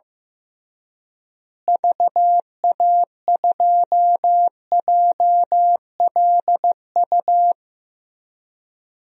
Morse code, heard as a keyed tone, about 15 wpm, 700 Hz. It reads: VA2JLU
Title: DXCC-100 Call Signs - 6 Characters - Repeated 3x - 15wpm